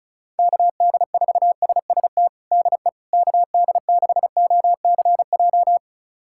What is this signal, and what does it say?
Morse code, heard as a keyed tone, about 36 wpm, 700 Hz. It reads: KD4SST DE KD6OCJ
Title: USA Callsign DE Callsign - 6 Characters - 35wpm